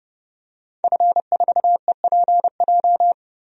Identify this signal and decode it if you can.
Morse code, heard as a keyed tone, about 30 wpm, 690 Hz. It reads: F4EPJ
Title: DXCC-100 Call Signs - 5 Characters - 30wpm